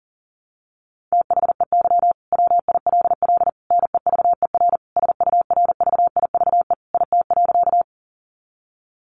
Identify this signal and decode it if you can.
Morse code, heard as a keyed tone, about 37 wpm, 700 Hz. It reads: THEY WILL NEVER SURVIVE IT.
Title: Sentences from Top 1300 Words 40wpm